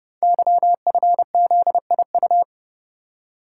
Morse code, heard as a keyed tone, about 30 wpm, 700 Hz. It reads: YFZIU
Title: Code Groups - 5 Letters - 30wpm